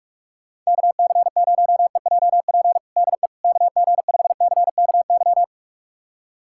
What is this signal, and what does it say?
Morse code, heard as a keyed tone, about 45 wpm, 700 Hz. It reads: KC0EJP DE KG5CKY
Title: USA Callsign DE Callsign - 6 Characters - 45wpm